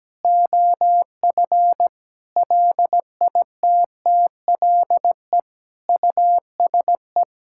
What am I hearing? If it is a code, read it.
Morse code, heard as a keyed tone, about 17 wpm, 700 Hz. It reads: OF LITTLE USE
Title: Sentences from Top 1000 Words - Review Part 5 - 17wpm